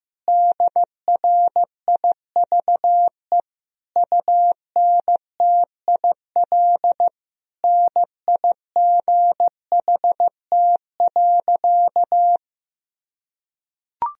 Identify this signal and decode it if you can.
Morse code, heard as a keyed tone, about 15 wpm, 700 Hz. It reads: DRIVE UNTIL NIGHT.
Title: Sentences from Top 1000 Words - Review Part 2 - 15wpm